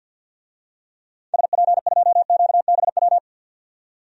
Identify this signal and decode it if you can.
Morse code, heard as a keyed tone, about 47 wpm, 700 Hz. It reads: SP2QBW
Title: DXCC-100 Call Signs - 6 Characters - 50wpm